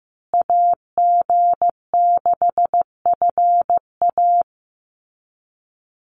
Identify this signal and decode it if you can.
Morse code, heard as a keyed tone, about 15 wpm, 700 Hz. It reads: AG6FA
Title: USA Call Signs Repeated 3x 15wpm